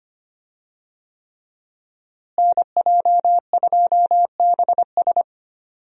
Morse code, heard as a keyed tone, about 25 wpm, 700 Hz. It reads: NJ2BS
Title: USA Call Signs - 5 Characters - Repeated 3x - 25wpm